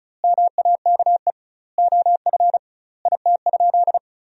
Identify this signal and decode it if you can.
Morse code, heard as a keyed tone, about 34 wpm, 700 Hz. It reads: MAKE OF IT?
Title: 2x Word-Spacing - Sentences from Top 100 Words - 35wpm